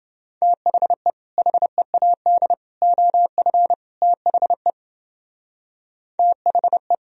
Morse code, heard as a keyed tone, about 30 wpm, 700 Hz. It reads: THE HEAD OF THE THE